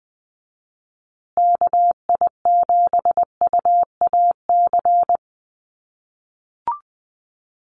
Morse code, heard as a keyed tone, about 20 wpm, 700 Hz. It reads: KI7UAC
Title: USA Call Signs Repeated 3x 20wpm